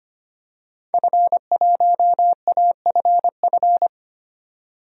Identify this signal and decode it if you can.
Morse code, heard as a keyed tone, about 25 wpm, 700 Hz. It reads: F1AFF
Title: DXCC-100 Call Signs - 5 Characters - 25wpm